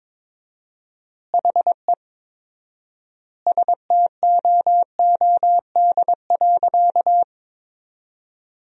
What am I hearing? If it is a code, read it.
Morse code, heard as a keyed tone, about 22 wpm, 700 Hz. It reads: HE STOOD.
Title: 4x Word-Spacing - Sentences from Top 500 Words - 22wpm